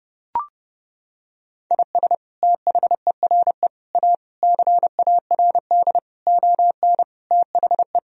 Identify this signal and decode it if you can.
Morse code, heard as a keyed tone, about 30 wpm, 700 Hz. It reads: IS THERE A CARD ON THE
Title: Sentences from Top 1000 Words - Review Part 4 - 30wpm